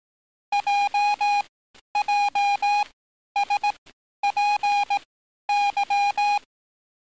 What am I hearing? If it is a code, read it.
Morse code, heard as a keyed tone, about 18 wpm, spaced as for 11 wpm, 820 Hz. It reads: JJSPY
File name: training / NAVPERS 11049to11060 / NAVPERS 11050 RA RB